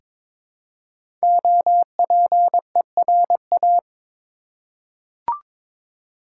Morse code, heard as a keyed tone, about 22 wpm, 700 Hz. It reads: OPERA
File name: 5 Letter Words - First 3 Letters Given - 22wpm